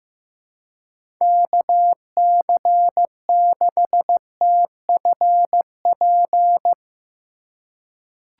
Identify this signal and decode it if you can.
Morse code, heard as a keyed tone, about 15 wpm, 700 Hz. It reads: KC6TFP
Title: USA Call Signs - 6 Characters - 3x Character Spacing - 15wpm